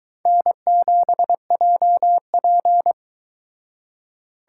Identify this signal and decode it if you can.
Morse code, heard as a keyed tone, about 23 wpm, 700 Hz. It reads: N7JP